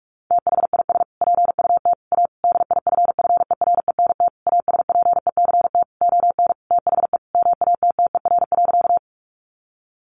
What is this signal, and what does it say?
Morse code, heard as a keyed tone, about 44 wpm, 700 Hz. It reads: THIS PUT A DIFFERENT ASPECT ON THE MATTER.
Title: Sentences from Top 1300 Words 45wpm